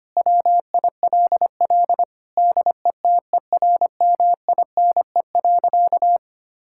Morse code, heard as a keyed tone, about 25 wpm, 700 Hz. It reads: WILL DETERMINE.
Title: Sentences from Top 1000 Words - Review Part 5 - 25wpm